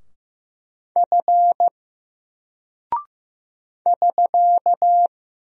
Morse code, heard as a keyed tone, about 15 wpm, 700 Hz. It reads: F <SK>
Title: Single Character (Letter:Number:Punctuation:Prosign) 15wpm